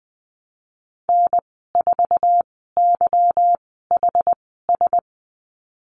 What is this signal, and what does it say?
Morse code, heard as a keyed tone, about 20 wpm, spaced as for 15 wpm, 700 Hz. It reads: N4YHS
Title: 200 Lesson 32g - Letter k - Call Signs 20wpm